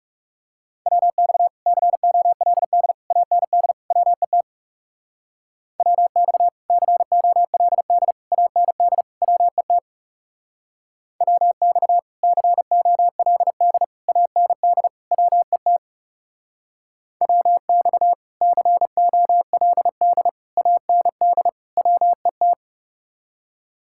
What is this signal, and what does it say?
Morse code, heard as a keyed tone, about 39 wpm, 700 Hz. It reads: WX COLD AND WET WX COLD AND WET WX COLD AND WET WX COLD AND WET